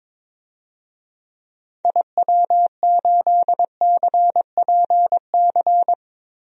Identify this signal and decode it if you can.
Morse code, heard as a keyed tone, about 22 wpm, 700 Hz. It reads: IW8CPC
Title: DXCC-100 Call Signs - 6 Characters - 22wpm